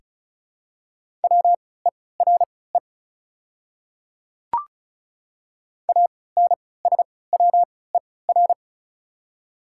Morse code, heard as a keyed tone, about 34 wpm, spaced as for 20 wpm, 700 Hz. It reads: WERE ANSWER
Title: 3x Character Spacing - Top 300 Words - 35wpm